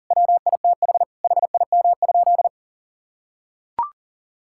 Morse code, heard as a keyed tone, about 40 wpm, 700 Hz. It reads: WITH HIM?
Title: Sentences from Top 1000 Words - Review Part 6 - 40wpm